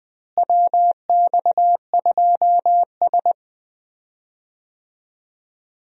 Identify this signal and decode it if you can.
Morse code, heard as a keyed tone, about 20 wpm, 700 Hz. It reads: WX2S
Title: USA Call Signs - 4 Characters - Repeated 3x - 20wpm